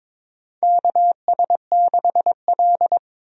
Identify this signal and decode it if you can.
Morse code, heard as a keyed tone, about 21 wpm, 700 Hz. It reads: KS6L
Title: USA Call Signs - 4 Characters - Repeated 2x - 22wpm